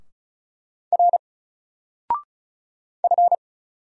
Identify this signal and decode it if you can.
Morse code, heard as a keyed tone, about 36 wpm, 700 Hz. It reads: R F
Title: Single Letter-Number Alt 35wpm